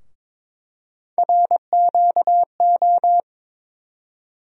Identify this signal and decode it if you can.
Morse code, heard as a keyed tone, about 21 wpm, 700 Hz. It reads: RQO
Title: USA Call Sign Postfixes - 3 Letters 22wpm